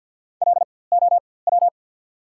Morse code, heard as a keyed tone, about 54 wpm, spaced as for 24 wpm, 690 Hz. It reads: LOW
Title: 4x Character Spacing - Top 300 Words - 50wpm